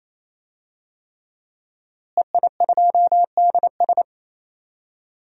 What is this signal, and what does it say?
Morse code, heard as a keyed tone, about 28 wpm, 700 Hz. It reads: EI2DS
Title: DXCC-100 Call Signs - 5 Characters - Repeated 3x - 28wpm